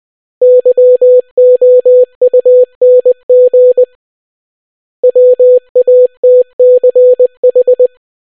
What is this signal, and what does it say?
Morse code, heard as a keyed tone, about 20 wpm, 500 Hz. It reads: YOUNG WATCH